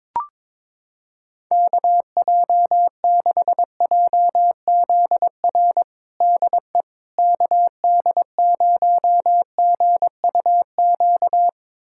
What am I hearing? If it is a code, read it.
Morse code, heard as a keyed tone, about 22 wpm, 700 Hz. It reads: KJ6JZR DE KD0GUQ
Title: USA Callsign DE Callsign 22wpm